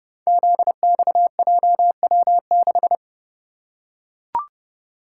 Morse code, heard as a keyed tone, about 30 wpm, 700 Hz. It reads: ZXJW6
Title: Code Groups - 5 Letter:Numbers - 30wpm